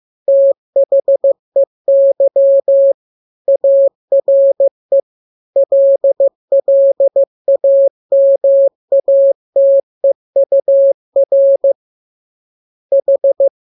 Morse code, heard as a keyed tone, about 15 wpm, 550 Hz. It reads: THEY ARE LLAMATEUR H
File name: jokes / joke 14 15 0000